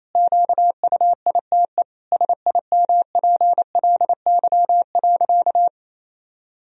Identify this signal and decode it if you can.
Morse code, heard as a keyed tone, about 28 wpm, 700 Hz. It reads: QUITE SIMPLY.
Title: Sentences from Top 1000 Words - Review Part 4 - 28wpm